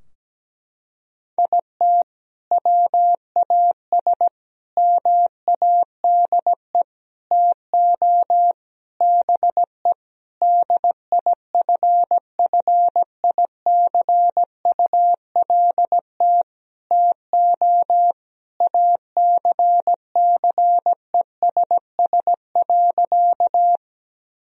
Morse code, heard as a keyed tone, about 17 wpm, 700 Hz. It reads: IT WAS MADE TO BE DIFFICULT TO ACCESS.
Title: Sentences from Top 1000 Words - Review Part 4 - 17wpm